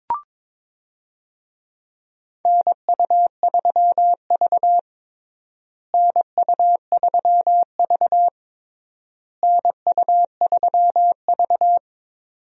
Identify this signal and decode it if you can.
Morse code, heard as a keyed tone, about 22 wpm, 700 Hz. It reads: NU3V NU3V NU3V
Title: USA Call Signs - 4 Characters - Repeated 3x - 22wpm